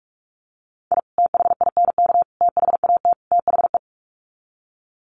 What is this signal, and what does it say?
Morse code, heard as a keyed tone, about 44 wpm, 700 Hz. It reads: I THINK THAT THE
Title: Sets of 4 Words 45wpm